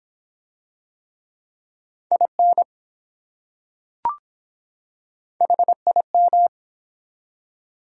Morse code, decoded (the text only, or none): IN HIM